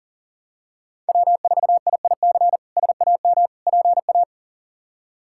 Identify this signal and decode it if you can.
Morse code, heard as a keyed tone, about 37 wpm, 700 Hz. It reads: W4IIC SAM PA
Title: K1USN SST QSO Practice - 40wpm